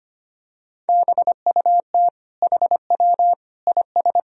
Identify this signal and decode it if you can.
Morse code, heard as a keyed tone, about 25 wpm, 700 Hz. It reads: BUT HW IS
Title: Sentences from Top 100 Words and Abbreviations 25wpm